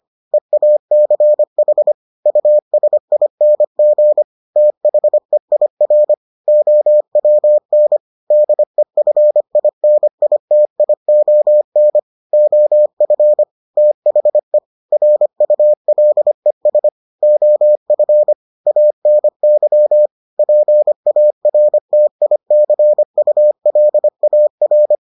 Morse code, heard as a keyed tone, about 25 wpm, 600 Hz. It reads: EACH USING THEIR OWN DEFINITION OF THE RULES OF ANY PARTICULAR